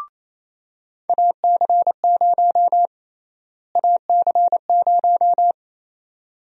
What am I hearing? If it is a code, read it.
Morse code, heard as a keyed tone, about 28 wpm, 700 Hz. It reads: AC0 AC0
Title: USA Call Sign Prefixes - Repeated 2x - 28wpm